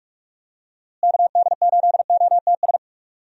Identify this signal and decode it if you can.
Morse code, heard as a keyed tone, about 46 wpm, 700 Hz. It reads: KD8OTS
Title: USA Call Signs - 6 Characters - Repeated 3x - 45wpm